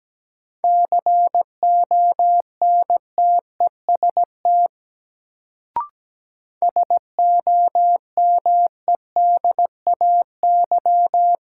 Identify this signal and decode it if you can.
Morse code, heard as a keyed tone, about 17 wpm, 700 Hz. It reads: CONTEST SOMEDAY
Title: Sentences from Top 1600 Words - New Word Drill - 17wpm